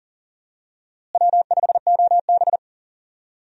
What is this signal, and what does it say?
Morse code, heard as a keyed tone, about 40 wpm, 700 Hz. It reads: W5OB
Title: USA Call Signs - 4 Characters - Repeated 2x - 40wpm